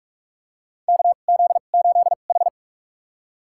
Morse code, heard as a keyed tone, about 46 wpm, spaced as for 32 wpm, 700 Hz. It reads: KZ8H
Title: USA Call Signs - 4 Characters - 2x Character Spacing - 45wpm